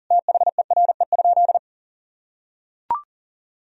Morse code, heard as a keyed tone, about 40 wpm, 700 Hz. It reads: THERE?
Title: Sentences from Top 1000 Words - Review Part 4 - 40wpm